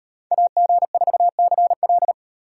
Morse code, heard as a keyed tone, about 35 wpm, 700 Hz. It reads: AG4CL